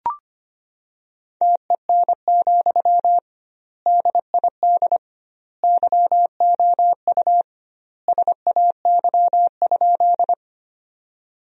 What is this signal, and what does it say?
Morse code, heard as a keyed tone, about 25 wpm, 700 Hz. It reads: TEN, DID YOU SAY?
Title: 2x Word-Spacing - Sentences from Top 400 Words - 25wpm